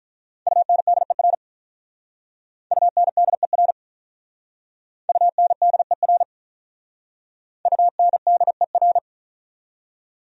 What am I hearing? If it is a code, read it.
Morse code, heard as a keyed tone, about 42 wpm, 700 Hz. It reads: UNDER UNDER UNDER UNDER
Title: Speed-Racing Top 500 Words 35wpm